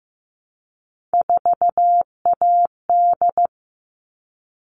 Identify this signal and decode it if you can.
Morse code, heard as a keyed tone, about 15 wpm, 700 Hz. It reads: 4AD
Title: All Cut Numbers - 3-Digit 15wpm